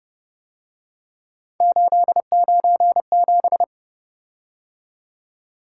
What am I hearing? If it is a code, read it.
Morse code, heard as a keyed tone, about 30 wpm, 700 Hz. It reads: 897